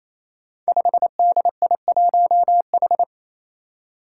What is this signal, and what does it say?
Morse code, heard as a keyed tone, about 28 wpm, 700 Hz. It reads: 5DI1H
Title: Five Challenging Letter-Number Combinations 28wpm